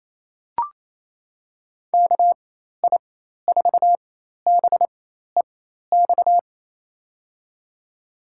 Morse code, heard as a keyed tone, about 28 wpm, spaced as for 13 wpm, 690 Hz. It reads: KI4BEX
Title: USA Call Signs - 6 Characters - 4x Character Spacing - 28wpm